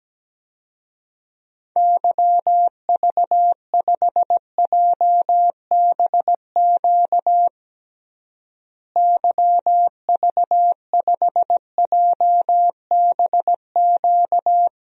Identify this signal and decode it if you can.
Morse code, heard as a keyed tone, about 17 wpm, 700 Hz. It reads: YV5JBQ YV5JBQ
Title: DXCC-100 Call Signs - 6 Characters - Repeated 2x - 17wpm